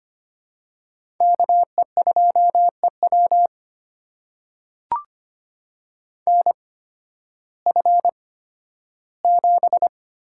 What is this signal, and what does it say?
Morse code, heard as a keyed tone, about 25 wpm, spaced as for 13 wpm, 700 Hz. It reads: KE2EW NF7
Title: USA Call Signs - 8x Character Spacing - 25wpm